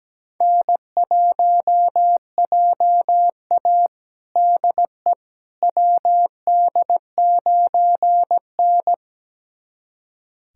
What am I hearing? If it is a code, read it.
Morse code, heard as a keyed tone, about 17 wpm, 700 Hz. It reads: N1JA DE WD9N
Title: USA Callsign DE Callsign - 4 Characters - 17wpm